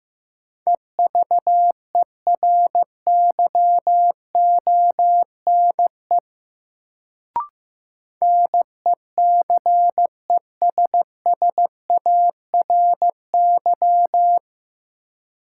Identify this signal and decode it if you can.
Morse code, heard as a keyed tone, about 15 wpm, 700 Hz. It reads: EVERYONE NECESSARY
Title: Sentences from Top 800 Words - New Word Drill - 15wpm